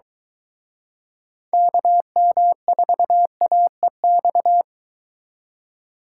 Morse code, heard as a keyed tone, about 24 wpm, 700 Hz. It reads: KM4AEX